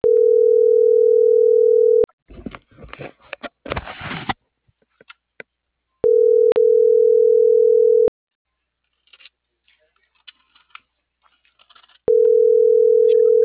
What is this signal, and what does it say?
Ringback tone (North America), 440 + 480 Hz.